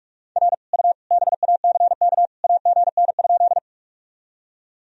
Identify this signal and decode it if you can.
Morse code, heard as a keyed tone, about 44 wpm, 700 Hz. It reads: R U BACK AGN?
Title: Sentences from Top 200 Words and Abbreviations 45wpm